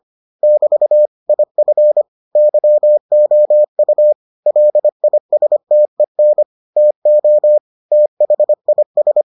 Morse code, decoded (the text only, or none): <BT> IF YOU LISTEN TO THIS